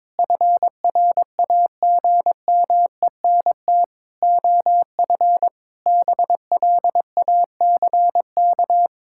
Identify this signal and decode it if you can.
Morse code, heard as a keyed tone, about 21 wpm, 700 Hz. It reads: FRAGMENT OF BLACK
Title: Pangram Challenge v2 - 22wpm